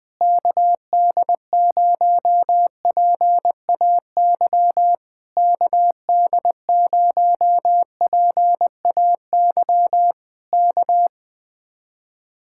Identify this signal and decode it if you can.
Morse code, heard as a keyed tone, about 20 wpm, 700 Hz. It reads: KD0PAY KD0PAY K
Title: Speed-Racing Common QSO Phrases - 20wpm